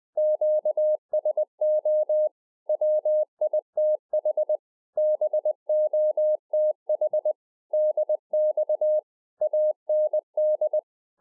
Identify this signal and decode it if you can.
Morse code, heard as a keyed tone, about 20 wpm, 610 Hz. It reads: QSO WITH BOTH DX AND